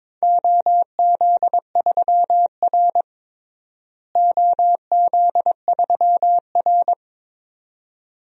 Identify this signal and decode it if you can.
Morse code, heard as a keyed tone, about 22 wpm, 700 Hz. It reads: OZ3R OZ3R
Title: DXCC-100 Call Signs - 4 Characters - Repeated 2x - 22wpm